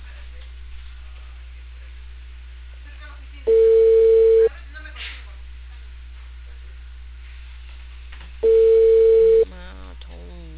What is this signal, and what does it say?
Ringback tone (China, 450 Hz), with speech.